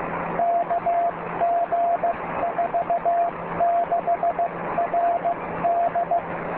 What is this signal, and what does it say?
Morse code, heard as a keyed tone, about 15 wpm, 690 Hz, with noise. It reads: KG46RD